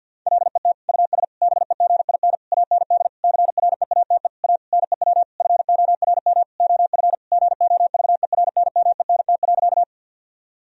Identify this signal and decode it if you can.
Morse code, heard as a keyed tone, about 50 wpm, 700 Hz. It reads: LET US BEGIN AND CREATE A NEW FORM OF GOVERNMENT.
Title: Sentences from Top 1000 Words - Review Part 3 - 50wpm